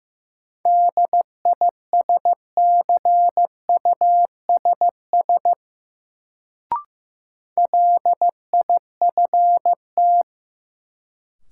Morse code, heard as a keyed tone, about 16 wpm, 700 Hz. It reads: DISCUSS LIFT
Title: Sentences from Top 1000 Words - New Word Drill - 15wpm